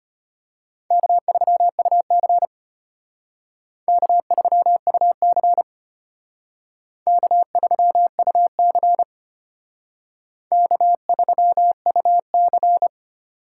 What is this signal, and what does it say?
Morse code, heard as a keyed tone, about 32 wpm, 700 Hz. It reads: K3UC K3UC K3UC K3UC